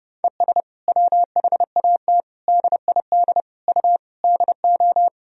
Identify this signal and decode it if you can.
Morse code, heard as a keyed tone, about 32 wpm, 700 Hz. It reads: ES WHAT DID U DO
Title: Sentences from Top 100 Words and Abbreviations 30wpm